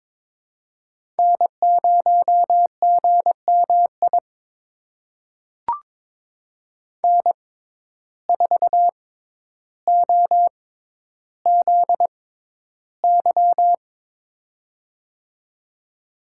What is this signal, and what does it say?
Morse code, heard as a keyed tone, about 22 wpm, spaced as for 11 wpm, 700 Hz. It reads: N0GMI N4OZY
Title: USA Call Signs - 5 Characters - 6x Character Spacing - 22wpm